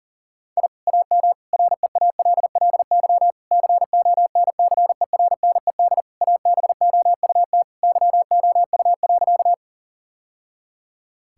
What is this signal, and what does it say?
Morse code, heard as a keyed tone, about 40 wpm, 700 Hz. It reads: I AM REALLY CONCERNED ABOUT YOU.
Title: Sentences from Top 1000 Words - Review Part 6 - 40wpm